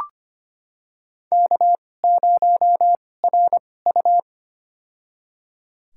Morse code, heard as a keyed tone, about 25 wpm, spaced as for 18 wpm, 700 Hz. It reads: K0RU
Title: USA Call Signs - 4 Characters - 2x Character Spacing - 25wpm